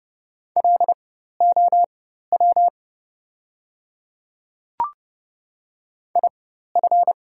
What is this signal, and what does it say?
Morse code, heard as a keyed tone, about 30 wpm, spaced as for 14 wpm, 690 Hz. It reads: LOW IF
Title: 4x Character Spacing - Top 200 Words - 30wpm